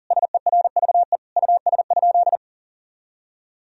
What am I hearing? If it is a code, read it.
Morse code, heard as a keyed tone, about 37 wpm, 700 Hz. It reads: SERVE US?
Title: Sentences from Top 1000 Words - Review Part 2 - 40wpm